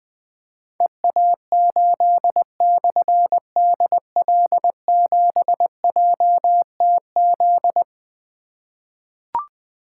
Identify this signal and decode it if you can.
Morse code, heard as a keyed tone, about 20 wpm, 700 Hz. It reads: EA8/DL7JTZ
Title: Difficult Call Signs 20wpm